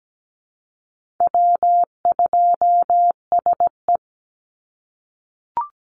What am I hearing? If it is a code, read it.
Morse code, heard as a keyed tone, about 17 wpm, 700 Hz. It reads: W2SE
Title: NA SOTA Chaser Callsigns 17wpm